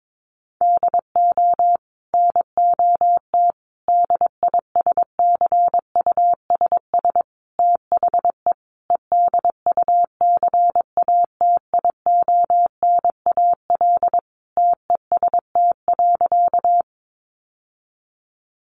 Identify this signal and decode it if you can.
Morse code, heard as a keyed tone, about 22 wpm, 700 Hz. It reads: DO NOT DISCUSS THE EDUCATIONAL TEST.
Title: Sentences from Top 1200 Words 22wpm